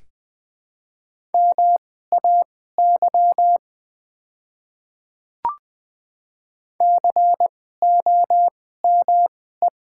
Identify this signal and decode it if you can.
Morse code, heard as a keyed tone, about 20 wpm, spaced as for 14 wpm, 700 Hz. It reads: MAY COME
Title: 2x Character Spacing - Top 200 Words - 20wpm